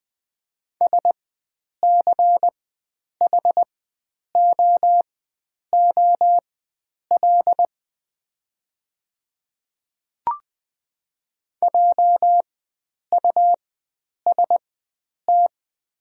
Morse code, heard as a keyed tone, about 20 wpm, spaced as for 9 wpm, 700 Hz. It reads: SCHOOL JUST
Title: 4x Character Spacing - Top 300 Words - 20wpm